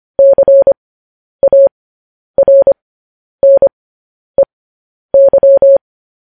Morse code, heard as a keyed tone, about 25 wpm, spaced as for 10 wpm, 550 Hz. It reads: CARNEY
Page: news-headlines-in-morse-code-at-10-wpm